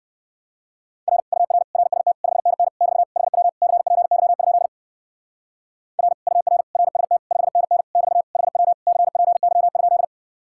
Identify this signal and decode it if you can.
Morse code, heard as a keyed tone, about 57 wpm, 700 Hz. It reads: R UR RST 5NN <BT> HW CPY? R UR RST 5NN <BT> HW CPY?